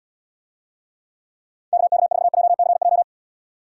Morse code, heard as a keyed tone, about 78 wpm, 700 Hz. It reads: KK4JCJ